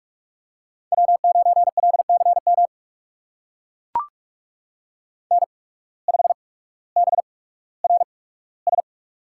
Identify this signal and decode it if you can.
Morse code, heard as a keyed tone, about 44 wpm, spaced as for 19 wpm, 700 Hz. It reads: W9LCM N5BRS
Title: USA Call Signs - 5 Characters - 8x Character Spacing - 45wpm